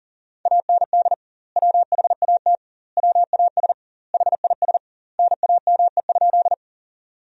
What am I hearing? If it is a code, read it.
Morse code, heard as a keyed tone, about 37 wpm, 700 Hz. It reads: AND WHAT WAS HIS NAME?
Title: 2x Word-Spacing - Sentences from Top 200 Words - 40wpm